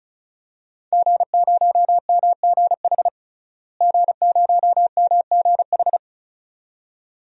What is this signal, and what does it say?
Morse code, heard as a keyed tone, about 36 wpm, 700 Hz. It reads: G0MGH G0MGH
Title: DXCC-100 Call Signs - 5 Characters - Repeated 2x - 35wpm